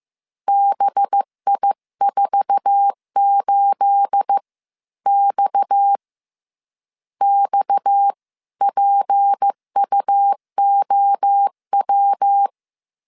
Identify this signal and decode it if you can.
Morse code, heard as a keyed tone, about 15 wpm, spaced as for 13 wpm, 800 Hz. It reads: BI48X XPUOW
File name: CW / DET / 1 9 1500C S C 14wpm 3245